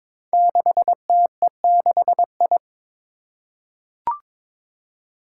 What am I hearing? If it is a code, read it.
Morse code, heard as a keyed tone, about 22 wpm, 700 Hz. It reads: 6TE6I
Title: Five Challenging Letter-Number Combinations 22wpm